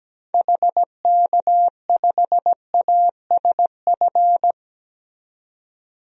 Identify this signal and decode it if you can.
Morse code, heard as a keyed tone, about 17 wpm, 700 Hz. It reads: HK5ASF
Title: DXCC-100 Call Signs - 6 Characters - 17wpm